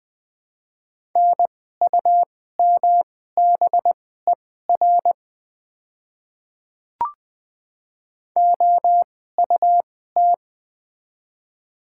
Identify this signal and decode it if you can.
Morse code, heard as a keyed tone, about 20 wpm, spaced as for 14 wpm, 700 Hz. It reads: NUMBER OUT
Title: 2x Character Spacing - Top 100 Words - 20wpm